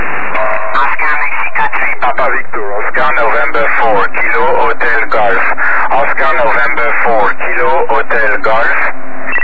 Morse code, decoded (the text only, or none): none